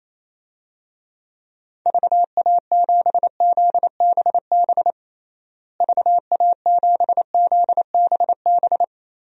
Morse code, heard as a keyed tone, about 28 wpm, 700 Hz. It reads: VA7ZBB VA7ZBB
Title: DXCC-100 Call Signs - 6 Characters - Repeated 2x - 28wpm